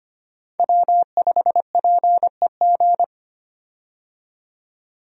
Morse code, heard as a keyed tone, about 25 wpm, 700 Hz. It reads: W5PEG
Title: USA Call Signs - 5 Characters - 25wpm